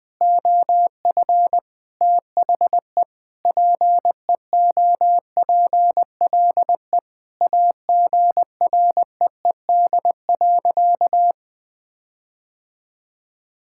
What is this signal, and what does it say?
Morse code, heard as a keyed tone, about 20 wpm, 700 Hz. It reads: OF THE PEOPLE AGREED.
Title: Sentences with 4-Word Phrases - 20wpm